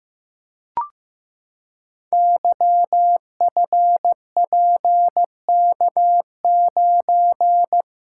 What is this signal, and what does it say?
Morse code, heard as a keyed tone, about 15 wpm, 700 Hz. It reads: YFPK9
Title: Code Groups - 5 Letter:Numbers - 15wpm